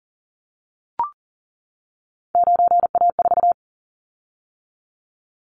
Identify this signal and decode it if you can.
Morse code, heard as a keyed tone, about 40 wpm, 700 Hz. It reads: 9A4
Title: Callsign Prefixes for Top 50 DXCC - 40wpm